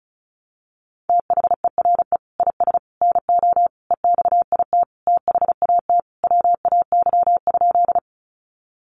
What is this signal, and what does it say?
Morse code, heard as a keyed tone, about 34 wpm, 700 Hz. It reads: THERE IS NO EXIT THAT WAY?
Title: Sentences from Top 1100 Words 35wpm